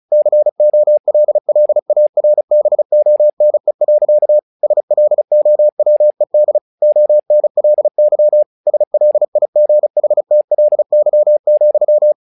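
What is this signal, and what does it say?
Morse code, heard as a keyed tone, about 34 wpm, 600 Hz. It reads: COLLARBONE. SLOWED ONLY SLIGHTLY,
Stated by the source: Morse practice text "Tower Trouble" sent at 35 WPM, from the SKCC - Straight Key Century Club learning center audio files.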